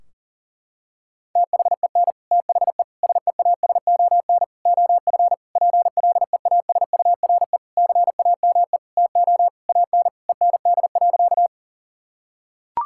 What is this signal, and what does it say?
Morse code, heard as a keyed tone, about 40 wpm, 700 Hz. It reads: THEN THE SEASON OF PLEASURE CAME TO AN END.
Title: Sentences from Top 1600 Words 40wpm